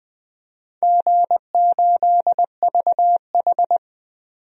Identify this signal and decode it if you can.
Morse code, heard as a keyed tone, about 20 wpm, 700 Hz. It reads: G8VH
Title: DXCC-100 Call Signs - 4 Characters - 20wpm